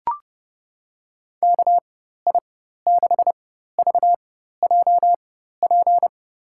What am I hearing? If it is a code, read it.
Morse code, heard as a keyed tone, about 30 wpm, spaced as for 14 wpm, 700 Hz. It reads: KI6VJP
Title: USA Call Signs - 6 Characters - 4x Character Spacing - 30wpm